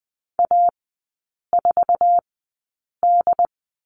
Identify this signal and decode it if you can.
Morse code, heard as a keyed tone, about 20 wpm, spaced as for 8 wpm, 700 Hz. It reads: A4D
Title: 081 Lesson 14e - Number 2 - Three Characters 20wpm